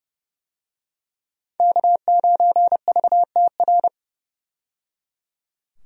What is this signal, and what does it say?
Morse code, heard as a keyed tone, about 30 wpm, 700 Hz. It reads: K9VTR
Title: USA Call Signs - 5 Characters - 30wpm